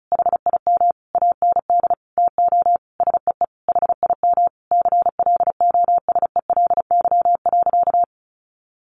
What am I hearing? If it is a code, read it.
Morse code, heard as a keyed tone, about 34 wpm, 700 Hz. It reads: HIM AND TO SEE HIM CLOSELY.
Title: Sentences from Top 1400 Words 35wpm